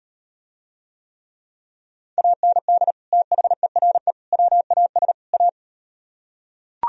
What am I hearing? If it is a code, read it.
Morse code, heard as a keyed tone, about 36 wpm, 700 Hz. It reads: AND THERE WAS A